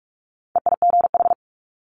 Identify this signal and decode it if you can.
Morse code, heard as a keyed tone, about 46 wpm, 700 Hz. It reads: EIGH
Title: Multi-letter Phonemes Quick Fire 45wpm